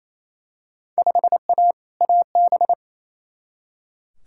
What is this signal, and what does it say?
Morse code, heard as a keyed tone, about 28 wpm, 700 Hz. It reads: 5A AB
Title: Field Day Exchanges Encore - 28wpm